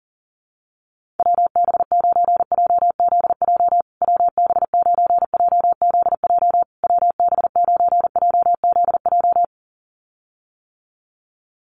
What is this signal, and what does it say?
Morse code, heard as a keyed tone, about 40 wpm, 700 Hz. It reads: WB9JZJ WB9JZJ WB9JZJ
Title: USA Call Signs Repeated 3x 40wpm